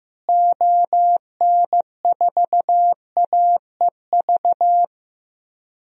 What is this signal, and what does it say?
Morse code, heard as a keyed tone, about 15 wpm, 700 Hz. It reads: ON4AEV
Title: DXCC-100 Call Signs - 6 Characters - Repeated 3x - 15wpm